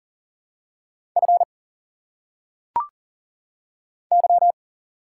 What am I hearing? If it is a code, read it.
Morse code, heard as a keyed tone, about 40 wpm, 700 Hz. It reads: F Y
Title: Single Letters - Phonetic Pronunciation 40wpm